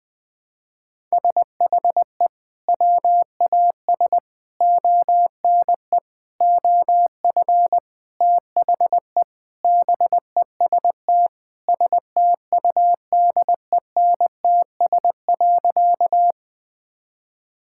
Morse code, heard as a keyed tone, about 20 wpm, 700 Hz. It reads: SHE WAS ONE OF THE BEST STUDENTS.
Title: Sentences with 4-Word Phrases - 20wpm